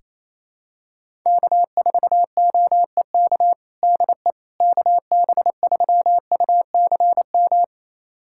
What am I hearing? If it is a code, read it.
Morse code, heard as a keyed tone, about 27 wpm, 700 Hz. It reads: K4OEK DE KB3UCM
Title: USA Callsign DE Callsign 28wpm